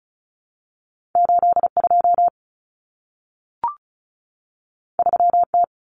Morse code, heard as a keyed tone, about 36 wpm, 700 Hz. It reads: 82 3T
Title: Common Cut Numbers - 2-Digit 35wpm